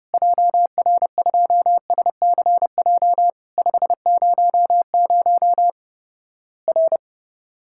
Morse code, heard as a keyed tone, about 29 wpm, 700 Hz. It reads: JR2SCJ 500 R
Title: Worked All Europe DX Contest - QTC Exchange - 30wpm